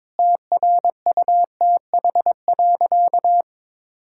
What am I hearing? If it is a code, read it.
Morse code, heard as a keyed tone, about 22 wpm, 700 Hz. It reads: TRUTH.
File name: Sentences with 7-Word Phrases - 22wpm